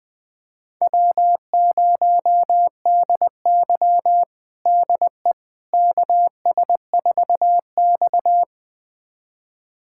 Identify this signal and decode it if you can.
Morse code, heard as a keyed tone, about 21 wpm, 700 Hz. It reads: W0DY DE KS4X
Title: USA Callsign DE Callsign - 4 Characters - 20wpm